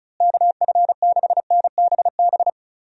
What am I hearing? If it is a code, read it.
Morse code, heard as a keyed tone, about 36 wpm, 700 Hz. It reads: KF6NBB
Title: USA Call Signs - 6 Characters - Repeated 3x - 35wpm